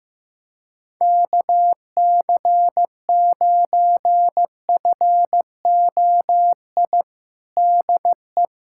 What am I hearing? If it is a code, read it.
Morse code, heard as a keyed tone, about 14 wpm, 700 Hz. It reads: KC9FOI DE
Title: USA Callsign DE Callsign - 6 Characters - 15wpm